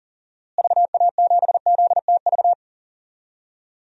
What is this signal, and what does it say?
Morse code, heard as a keyed tone, about 40 wpm, 700 Hz. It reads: VA7ZTV
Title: DXCC-100 Call Signs - 6 Characters - Repeated 2x - 40wpm